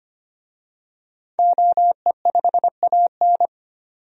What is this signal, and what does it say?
Morse code, heard as a keyed tone, about 25 wpm, 700 Hz. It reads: OE5AN
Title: DXCC-100 Call Signs - 5 Characters - 25wpm